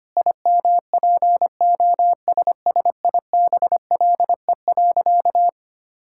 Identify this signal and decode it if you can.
Morse code, heard as a keyed tone, about 25 wpm, 700 Hz. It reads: IMPOSSIBLE.
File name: practice / 25wpm / Sentences with 4-Word Phrases - 25wpm